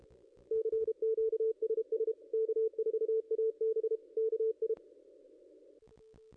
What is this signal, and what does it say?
Morse code, heard as a keyed tone, about 29 wpm, 440 Hz. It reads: CQSS K4AB KI
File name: contests / ss-cw-k5tr-op-n5rz -0600